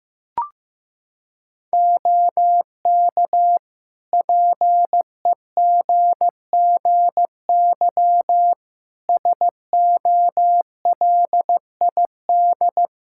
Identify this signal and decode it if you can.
Morse code, heard as a keyed tone, about 15 wpm, 700 Hz. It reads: OK PEGGY SOLID
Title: Practice QSO - Exchange 3 - More Info - 15wpm